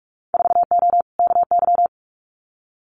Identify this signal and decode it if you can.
Morse code, heard as a keyed tone, about 46 wpm, 700 Hz. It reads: 4O KY
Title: Winter Field Day Exchanges Encore 45wpm